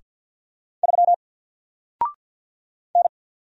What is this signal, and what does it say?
Morse code, heard as a keyed tone, about 50 wpm, 700 Hz. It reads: 3 N
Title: Single Letter-Number Alt 50wpm